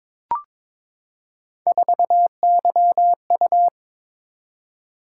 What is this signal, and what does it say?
Morse code, heard as a keyed tone, about 22 wpm, 700 Hz. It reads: 4YU
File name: practice / 22wpm / Three Letter-Number Combinations - 22wpm